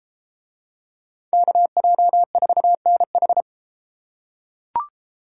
Morse code, heard as a keyed tone, about 33 wpm, 700 Hz. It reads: KJ4NH